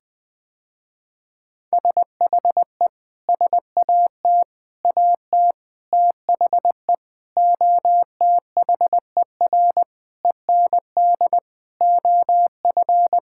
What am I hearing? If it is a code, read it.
Morse code, heard as a keyed tone, about 20 wpm, 700 Hz. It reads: SHE SAT AT THE OTHER END OF
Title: Sentences with 6-Word Phrases - Spoken Beginning - 20wpm